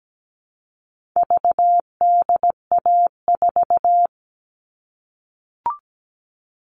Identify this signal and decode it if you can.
Morse code, heard as a keyed tone, about 17 wpm, 700 Hz. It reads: VDA4
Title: All Cut Numbers - 4-Digit 17wpm